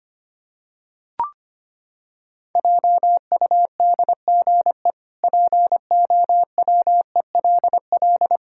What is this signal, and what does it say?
Morse code, heard as a keyed tone, about 25 wpm, 700 Hz. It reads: JUDGE POWELL
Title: Pangram Challenge v2 - 25wpm